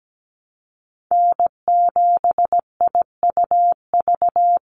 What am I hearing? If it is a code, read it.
Morse code, heard as a keyed tone, about 17 wpm, 700 Hz. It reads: N7IUV
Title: USA Call Signs 17wpm